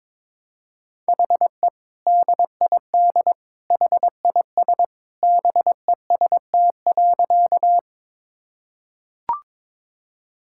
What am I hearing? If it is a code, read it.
Morse code, heard as a keyed tone, about 22 wpm, 700 Hz. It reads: HE DID HIS BEST.
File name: practice / 22wpm / Sentences with 2-Word Phrases - 22wpm